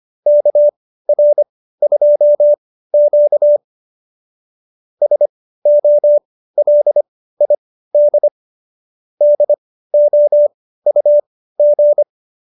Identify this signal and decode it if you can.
Morse code, heard as a keyed tone, about 25 wpm, spaced as for 15 wpm, 590 Hz. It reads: KR2Q SOLID DOUG